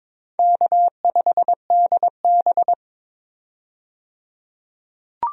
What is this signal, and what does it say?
Morse code, heard as a keyed tone, about 22 wpm, 700 Hz. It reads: K5DB
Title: USA Call Signs - 4 Characters - 22wpm